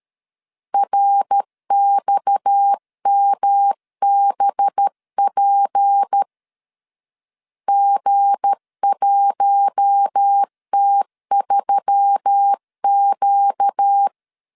Morse code, heard as a keyed tone, about 13 wpm, 800 Hz. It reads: RXMBP G1T3Q
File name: LEZIONI / 7 GF9Z / 7 GF 12wpm 2122